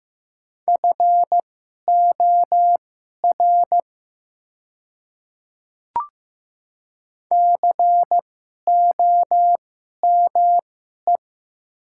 Morse code, heard as a keyed tone, about 15 wpm, spaced as for 11 wpm, 700 Hz. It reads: FOR COME
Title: 2x Character Spacing - Top 100 Words - 15wpm